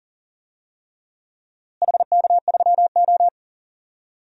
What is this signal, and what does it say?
Morse code, heard as a keyed tone, about 39 wpm, 700 Hz. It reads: HK3O
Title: DXCC-100 Call Signs - 4 Characters - Repeated 3x - 40wpm